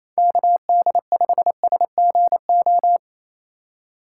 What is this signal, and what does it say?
Morse code, heard as a keyed tone, about 27 wpm, 700 Hz. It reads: KD5SGO
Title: USA Call Signs - 6 Characters - 6x Character Spacing - 28wpm